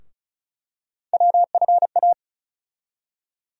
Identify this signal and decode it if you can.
Morse code, heard as a keyed tone, about 36 wpm, 700 Hz. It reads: WFA
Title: USA Call Sign Postfixes - 3 Letters 35wpm